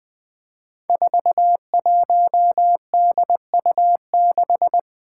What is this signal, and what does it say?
Morse code, heard as a keyed tone, about 20 wpm, 700 Hz. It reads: 41DU6
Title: Five Challenging Letter-Number Combinations 20wpm